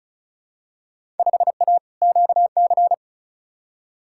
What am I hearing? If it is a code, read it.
Morse code, heard as a keyed tone, about 36 wpm, 700 Hz. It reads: 5A QC
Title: Field Day Exchanges Encore - 35wpm